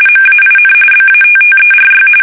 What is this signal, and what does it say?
RTTY, about 45.45 baud. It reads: MAIN BATTERY
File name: fsk850 / battleshipnj